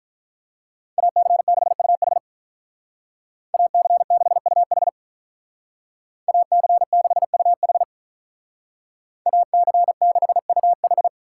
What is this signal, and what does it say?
Morse code, heard as a keyed tone, about 47 wpm, 700 Hz. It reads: AC6UH AC6UH AC6UH AC6UH